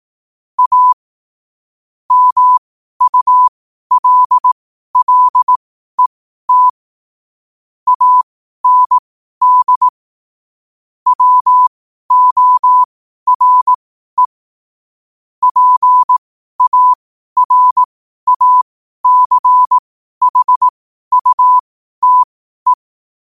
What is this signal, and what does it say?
Morse code, heard as a keyed tone, about 18 wpm, spaced as for 12 wpm, 1000 Hz. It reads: A MULLET AND WORE PARACHUTE
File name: qotdfiles / 12wpm